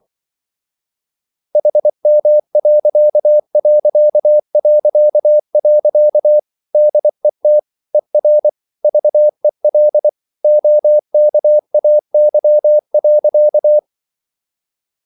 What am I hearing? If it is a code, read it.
Morse code, heard as a keyed tone, about 24 wpm, 600 Hz. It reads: HM.... DET ER VEL OKAY.
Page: Ferie 24 wpm